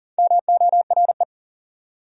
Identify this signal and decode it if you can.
Morse code, heard as a keyed tone, about 40 wpm, 700 Hz. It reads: MORE
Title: Top 100 Words 40wpm